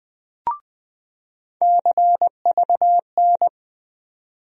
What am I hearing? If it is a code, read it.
Morse code, heard as a keyed tone, about 20 wpm, 700 Hz. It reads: CVN
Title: USA Call Sign Postfixes - 3 Letters 20wpm